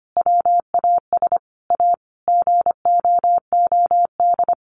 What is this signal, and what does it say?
Morse code, heard as a keyed tone, about 25 wpm, 700 Hz. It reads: WAS A GOOD
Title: Sentences from Top 1300 Words 25wpm